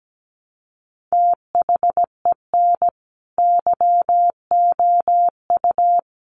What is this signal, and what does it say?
Morse code, heard as a keyed tone, about 17 wpm, 700 Hz. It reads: THEN YOU
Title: Sentences from Top 400 Words 17wpm